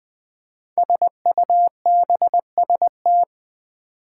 Morse code, heard as a keyed tone, about 20 wpm, 700 Hz. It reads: SUBST
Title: Build A Word - 5 Letters At A Time - 12-Letter Words - 20wpm